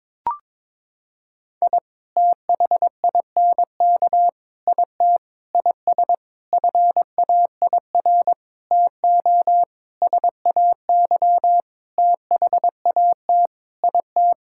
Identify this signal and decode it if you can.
Morse code, heard as a keyed tone, about 21 wpm, 700 Hz. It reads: I THINK IT IS FAIR TO SAY THAT IT
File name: practice / 22wpm / Sentences with 8-Word Phrases - 22wpm